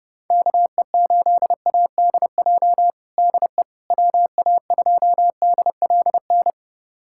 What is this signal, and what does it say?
Morse code, heard as a keyed tone, about 32 wpm, 700 Hz. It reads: KE8ADJ DE WA2DLN
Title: USA Callsign DE Callsign - 6 Characters - 30wpm